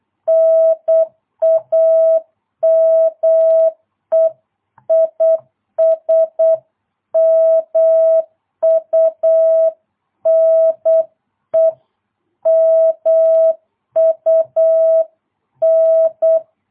Morse code, heard as a keyed tone, about 8 wpm, 650 Hz. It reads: NAME IS MUNE MUN